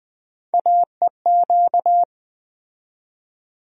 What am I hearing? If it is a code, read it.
Morse code, heard as a keyed tone, about 20 wpm, 700 Hz. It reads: AEQ
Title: USA Call Sign Postfixes - 3 Letters 20wpm